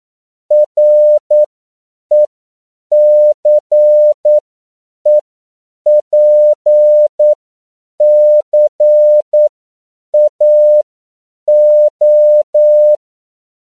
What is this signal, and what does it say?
Morse code, heard as a keyed tone, about 9 wpm, spaced as for 7 wpm, 600 Hz. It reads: RECEPCAO
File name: CW / 10PPM